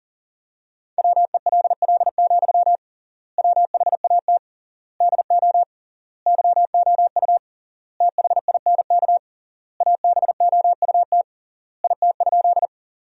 Morse code, heard as a keyed tone, about 40 wpm, 700 Hz. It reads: WELL, WHAT DO YOU THINK ABOUT IT?
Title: 3x Word-Spacing - Sentences from Top 200 Words - 40wpm